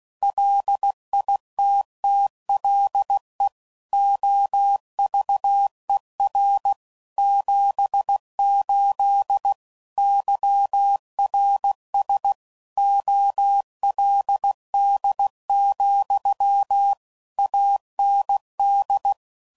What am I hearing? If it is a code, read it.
Morse code, heard as a keyed tone, about 16 wpm, 800 Hz. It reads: LITTLE OVER 78 YRS OLD, AND
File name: CW Code Groups @15WPM / QSOs / kc7haa